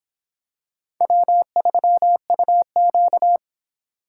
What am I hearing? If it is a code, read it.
Morse code, heard as a keyed tone, about 25 wpm, 700 Hz. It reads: W3UQ